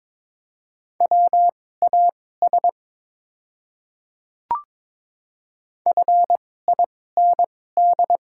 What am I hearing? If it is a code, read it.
Morse code, heard as a keyed tone, about 22 wpm, spaced as for 16 wpm, 700 Hz. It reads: WAS FIND
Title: 2x Character Spacing - Top 100 Words - 22wpm